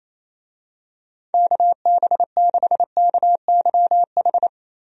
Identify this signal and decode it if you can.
Morse code, heard as a keyed tone, about 28 wpm, 700 Hz. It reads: KB6KYH
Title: USA Call Signs - 6 Characters - Repeated 2x - 28wpm